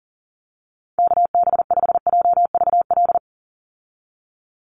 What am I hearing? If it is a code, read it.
Morse code, heard as a keyed tone, about 40 wpm, 700 Hz. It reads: KB5JVL
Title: USA Call Signs 40wpm